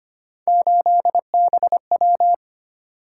Morse code, heard as a keyed tone, about 29 wpm, 700 Hz. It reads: 8BW
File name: Speed-Racing - Three Letter-Number Combinations - 25wpm